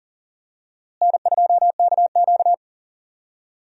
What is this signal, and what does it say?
Morse code, heard as a keyed tone, about 40 wpm, 700 Hz. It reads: N2KQ
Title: USA Call Signs - 4 Characters - Repeated 2x - 40wpm